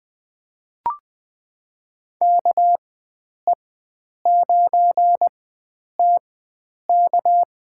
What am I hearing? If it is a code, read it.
Morse code, heard as a keyed tone, about 20 wpm, spaced as for 9 wpm, 700 Hz. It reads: KE9TK
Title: USA Call Signs - 4x Character Spacing - 20wpm